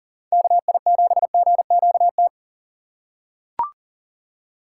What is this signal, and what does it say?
Morse code, heard as a keyed tone, about 40 wpm, 700 Hz. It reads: KI7GQT
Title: USA Call Signs - 6 Characters - 6x Character Spacing - 40wpm